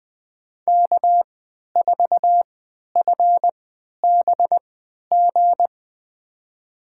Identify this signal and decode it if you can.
Morse code, heard as a keyed tone, about 20 wpm, spaced as for 11 wpm, 700 Hz. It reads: K4FBG
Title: USA Call Signs - 5 Characters - 3x Character Spacing - 20wpm